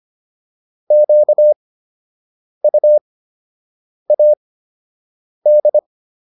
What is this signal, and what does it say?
Morse code, heard as a keyed tone, about 25 wpm, spaced as for 7 wpm, 600 Hz. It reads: QUAD